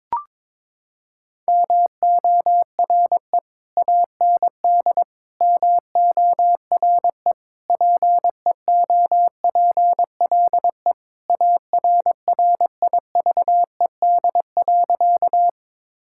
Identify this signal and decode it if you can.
Morse code, heard as a keyed tone, about 22 wpm, 700 Hz. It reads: MORE AND MORE PEOPLE ARRIVED.
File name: practice / 22wpm / Sentences with 3-Word Phrases - 22wpm